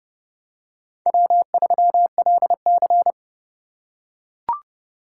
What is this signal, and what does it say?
Morse code, heard as a keyed tone, about 30 wpm, 700 Hz. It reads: W3LC
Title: USA Call Signs - 4 Characters - 4x Character Spacing - 30wpm